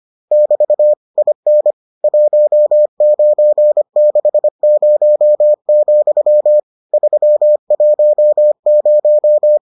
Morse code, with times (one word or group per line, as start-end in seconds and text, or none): 0.3-0.9 <BT>
1.2-1.7 IN
2.0-6.6 1960,
6.9-9.6 310